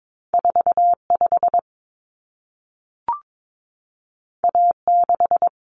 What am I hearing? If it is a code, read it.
Morse code, heard as a keyed tone, about 22 wpm, 700 Hz. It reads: 45 A6
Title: Common Cut Numbers - 2-Digit 22wpm